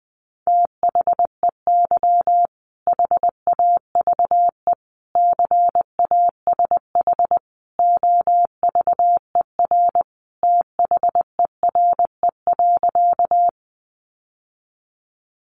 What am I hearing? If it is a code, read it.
Morse code, heard as a keyed tone, about 20 wpm, 700 Hz. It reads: THEY HAVE CASH OVER THERE.
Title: Sentences from Top 700 Words 20wpm